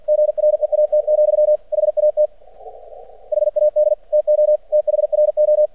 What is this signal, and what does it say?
Morse code, heard as a keyed tone, about 50 wpm, 600 Hz. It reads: GREAT, HAT HAD TO THRO
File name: archived / CRSnet